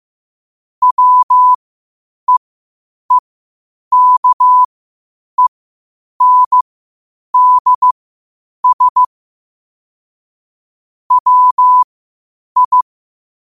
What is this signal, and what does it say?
Morse code, heard as a keyed tone, about 15 wpm, spaced as for 8 wpm, 1000 Hz. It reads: WEEKENDS WI
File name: qotdfiles / 8wpm